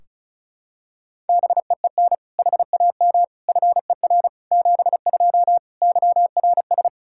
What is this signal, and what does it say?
Morse code, heard as a keyed tone, about 34 wpm, 700 Hz. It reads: BEEN HAM FER 72 YRS
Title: Practice QSO - Exchange 3 - More Info - 35wpm